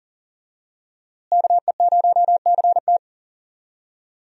Morse code, heard as a keyed tone, about 40 wpm, 700 Hz. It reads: KE0CT
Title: USA Call Signs - 5 Characters - Repeated 3x - 40wpm